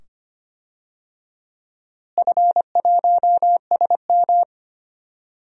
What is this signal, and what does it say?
Morse code, heard as a keyed tone, about 25 wpm, 700 Hz. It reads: F1SM
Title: DXCC-100 Call Signs - 4 Characters - Repeated 2x - 25wpm